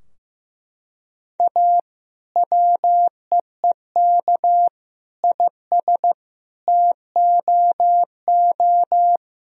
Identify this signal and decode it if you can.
Morse code, heard as a keyed tone, about 15 wpm, 700 Hz. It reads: A WEEK IS TOO
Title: Sentences from Top 1000 Words - Review Part 2 - 15wpm